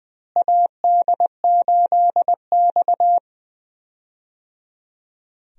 Morse code, heard as a keyed tone, about 20 wpm, 700 Hz. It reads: AD8X
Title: USA Call Signs - 4 Characters - Repeated 3x - 20wpm